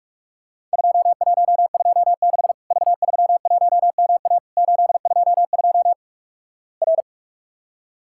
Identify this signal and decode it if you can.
Morse code, heard as a keyed tone, about 48 wpm, 700 Hz. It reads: 2126 V31MA 822 R
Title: Worked All Europe DX Contest - QTC Exchange - 45wpm